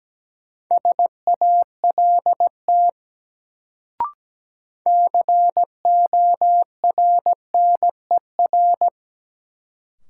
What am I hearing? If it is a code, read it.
Morse code, heard as a keyed tone, about 17 wpm, 700 Hz. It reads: SALT CORNER
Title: Sets of 5 Words - New Word Drill - 17wpm